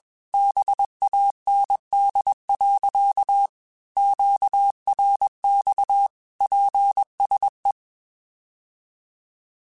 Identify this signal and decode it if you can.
Morse code, heard as a keyed tone, about 21 wpm, 800 Hz. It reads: BAND. QRX PSE
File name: QSOs / QSOs-mp3s / kc7sac@20